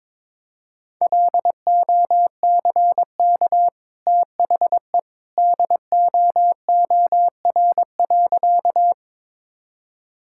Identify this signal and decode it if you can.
Morse code, heard as a keyed tone, about 21 wpm, 700 Hz. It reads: LOCK THE DOOR.
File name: practice / 22wpm / Easy Sentences with Hints - 22wpm